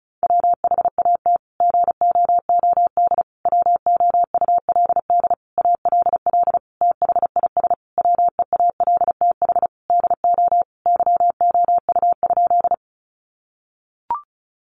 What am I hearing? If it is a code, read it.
Morse code, heard as a keyed tone, about 34 wpm, 700 Hz. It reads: WHAT GOOD WOULD ALL THIS WEALTH DO YOU?
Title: Sentences from Top 1200 Words 35wpm